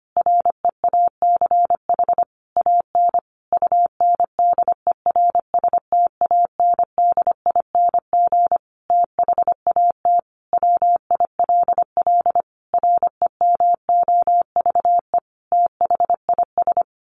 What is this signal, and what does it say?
Morse code, heard as a keyed tone, about 25 wpm, 700 Hz. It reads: REACH AN UNDERSTANDING THAT WILL REMOVE THIS
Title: Sentences from Top 1500 Words 25wpm